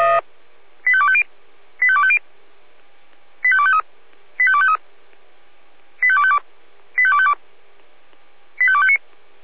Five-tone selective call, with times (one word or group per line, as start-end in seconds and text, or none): 0.9-1.2 85250
1.8-2.2 85250
3.4-3.8 85253
4.4-4.8 85253
6.0-6.4 85252
7.0-7.3 85252
8.6-9.0 85259